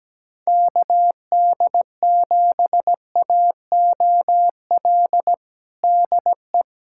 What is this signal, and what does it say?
Morse code, heard as a keyed tone, about 17 wpm, 700 Hz. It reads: KD7AOL DE
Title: USA Callsign DE Callsign 17wpm